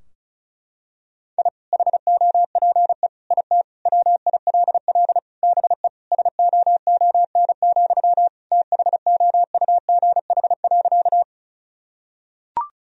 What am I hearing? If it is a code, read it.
Morse code, heard as a keyed tone, about 36 wpm, 700 Hz. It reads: I HOPE IT WILL BE SOON, THOUGH.
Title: Sentences from Top 1000 Words - Review Part 2 - 35wpm